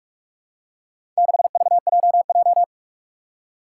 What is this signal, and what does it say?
Morse code, heard as a keyed tone, about 46 wpm, 700 Hz. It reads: 6VJJ
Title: Four Challenging Letter-Number Combinations 45wpm